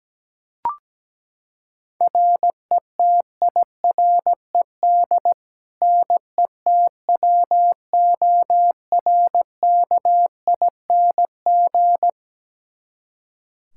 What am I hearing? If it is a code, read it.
Morse code, heard as a keyed tone, about 17 wpm, 700 Hz. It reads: RETIRED NETWORKING
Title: K1USN SST QSO Practice 17wpm